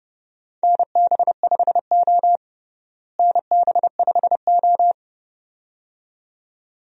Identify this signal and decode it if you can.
Morse code, heard as a keyed tone, about 32 wpm, 700 Hz. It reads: NB5O NB5O
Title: USA Call Signs - 4 Characters - Repeated 2x - 30wpm